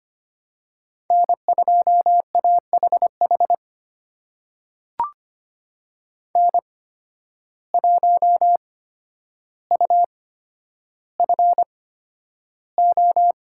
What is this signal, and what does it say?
Morse code, heard as a keyed tone, about 25 wpm, spaced as for 11 wpm, 700 Hz. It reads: N2AHH N1UFO
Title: USA Call Signs - 5 Characters - 8x Character Spacing - 25wpm